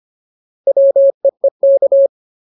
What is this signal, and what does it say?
Morse code, heard as a keyed tone, about 25 wpm, 550 Hz. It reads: WEEK
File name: Ultimate Word List / 25WPM Ultimate